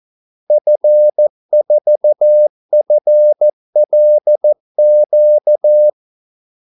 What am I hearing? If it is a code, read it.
Morse code, heard as a keyed tone, about 14 wpm, 600 Hz. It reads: F4FLQ
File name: Callsigns-14wpm